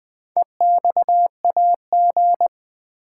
Morse code, heard as a keyed tone, about 20 wpm, 700 Hz. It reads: EXAG
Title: Build A Word - 4 Letters At A Time - 10-Letter Words - 20wpm